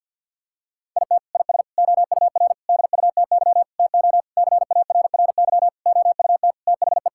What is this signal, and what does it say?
Morse code, heard as a keyed tone, about 50 wpm, 700 Hz. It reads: IT IS OUR DUTY TO CARRY OUT THE
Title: Sentences from Top 1600 Words 50wpm